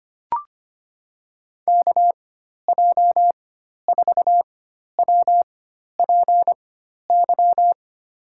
Morse code, heard as a keyed tone, about 25 wpm, spaced as for 12 wpm, 700 Hz. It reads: KJ4WPY
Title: USA Call Signs - 6 Characters - 4x Character Spacing - 25wpm